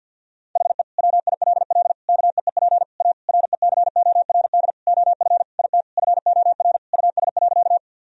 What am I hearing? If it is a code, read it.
Morse code, heard as a keyed tone, about 50 wpm, 700 Hz. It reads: HE WILL KEEP A RECORD OF IT FOR US.
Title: Sentences from Top 1000 Words - Review Part 3 - 50wpm